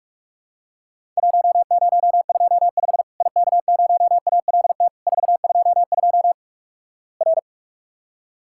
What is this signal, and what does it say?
Morse code, heard as a keyed tone, about 46 wpm, 700 Hz. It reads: 1025 IK0ALT 422 R
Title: Worked All Europe DX Contest - QTC Exchange - 45wpm